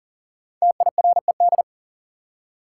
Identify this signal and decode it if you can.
Morse code, heard as a keyed tone, about 40 wpm, 700 Hz. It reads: TIRED
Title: Five Letter Words - 40wpm